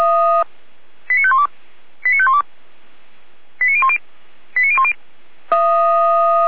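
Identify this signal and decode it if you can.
Five-tone selective call (ZVEI), heard as quick runs of five tones, tones of about 70 ms. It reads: 89512 89512 89010 89010
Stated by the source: ZVEI five-tone selective call received with an SDR, archived by Feuerwehr Orlamünde.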